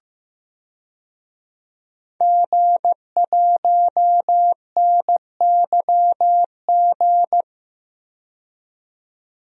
Morse code, heard as a keyed tone, about 15 wpm, 700 Hz. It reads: G1NYG
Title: DXCC-100 Call Signs - 5 Characters - 15wpm